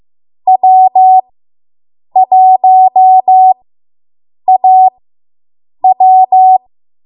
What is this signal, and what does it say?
Morse code, heard as a keyed tone, about 15 wpm, spaced as for 7 wpm, 750 Hz. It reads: W1AW